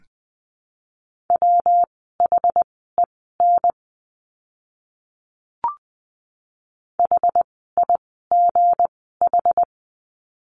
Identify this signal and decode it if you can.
Morse code, heard as a keyed tone, about 20 wpm, spaced as for 15 wpm, 700 Hz. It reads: WHEN HIGH
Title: 192 Lesson 31f - Letter v - Words 20wpm